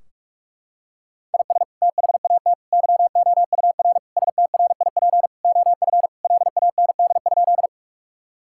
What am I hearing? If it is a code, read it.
Morse code, heard as a keyed tone, about 46 wpm, 700 Hz. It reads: IS THAT YOUR STRIP OF LAND?
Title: Sentences from Top 1600 Words 45wpm